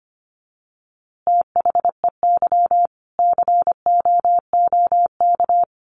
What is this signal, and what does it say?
Morse code, heard as a keyed tone, about 25 wpm, 700 Hz. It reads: THEY COOK
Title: Easy Phrases 25wpm